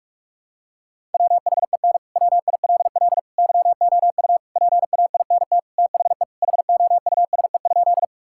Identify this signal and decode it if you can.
Morse code, heard as a keyed tone, about 46 wpm, 700 Hz. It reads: WHEN WILL YOU PAINT THE HOUSE?
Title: Sentences from Top 1000 Words - Review Part 6 - 45wpm